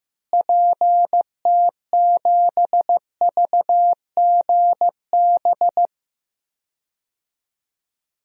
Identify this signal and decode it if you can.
Morse code, heard as a keyed tone, about 16 wpm, 700 Hz. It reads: PT7VGB
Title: DXCC-100 Call Signs - 6 Characters - Repeated 2x - 15wpm